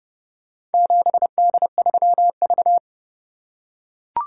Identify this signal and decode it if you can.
Morse code, heard as a keyed tone, about 29 wpm, 700 Hz. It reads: 7D3V